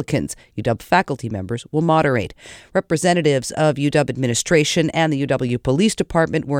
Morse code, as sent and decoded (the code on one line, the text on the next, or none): none